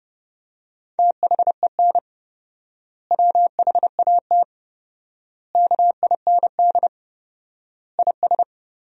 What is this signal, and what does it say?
Morse code, heard as a keyed tone, about 30 wpm, 700 Hz. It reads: THEN WHAT KIND IS
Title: 4x Word-Spacing - Sentences from Top 200 Words - 30wpm